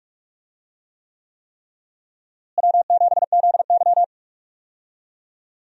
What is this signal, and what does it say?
Morse code, heard as a keyed tone, about 46 wpm, 700 Hz. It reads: W7ZY
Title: USA Call Signs - 4 Characters - Repeated 3x - 45wpm